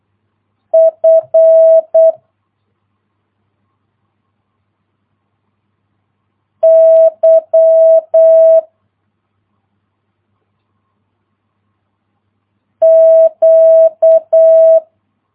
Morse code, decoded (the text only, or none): F Y Q